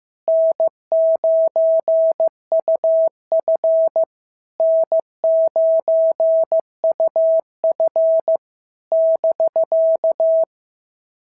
Morse code, heard as a keyed tone, about 15 wpm, 650 Hz. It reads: N9UF N9UF <BK>
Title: SOTA QSO Practice 15wpm